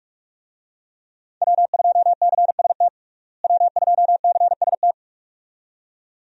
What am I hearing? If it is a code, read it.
Morse code, heard as a keyed tone, about 41 wpm, 700 Hz. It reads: W2CST W2CST
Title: USA Call Signs - 5 Characters - Repeated 2x - 45wpm